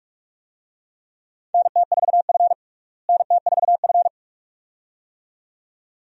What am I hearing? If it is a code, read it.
Morse code, heard as a keyed tone, about 48 wpm, 700 Hz. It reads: NT4F NT4F
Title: USA Call Signs - 4 Characters - Repeated 2x - 45wpm